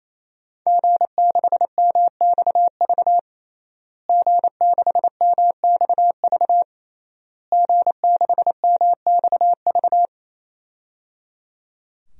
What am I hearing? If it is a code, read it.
Morse code, heard as a keyed tone, about 28 wpm, 700 Hz. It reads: G6MXV G6MXV G6MXV
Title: DXCC-100 Call Signs - 5 Characters - Repeated 3x - 28wpm